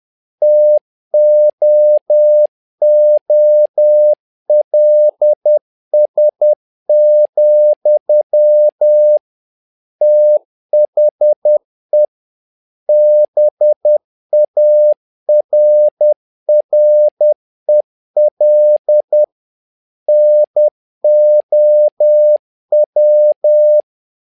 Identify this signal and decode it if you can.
Morse code, heard as a keyed tone, about 10 wpm, 600 Hz. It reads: TOOLS, THE BARREL NOW
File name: learning center / tower trouble / tower trouble 10wpm